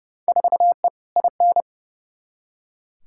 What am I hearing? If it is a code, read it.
Morse code, heard as a keyed tone, about 30 wpm, 700 Hz. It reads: 4E IN
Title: Field Day Exchanges - 30wpm